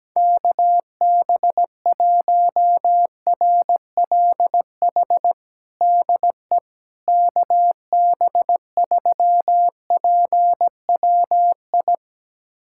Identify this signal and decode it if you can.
Morse code, heard as a keyed tone, about 17 wpm, 700 Hz. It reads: KB1RLH DE KB3PWI
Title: USA Callsign DE Callsign 17wpm